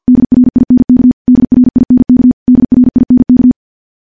Morse code, heard as a keyed tone, about 50 wpm, 250 Hz. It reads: DRINK DRINK DRINK
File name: morse / morse-training-50WPM